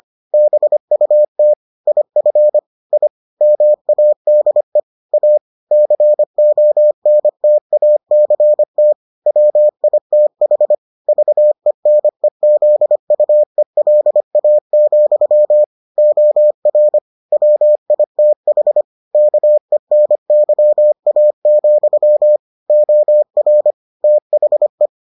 Morse code, heard as a keyed tone, about 25 wpm, 600 Hz. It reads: BUT IF I MADE A CONTACT WITH VENEZUELA, OR WITH KENYA, OR THE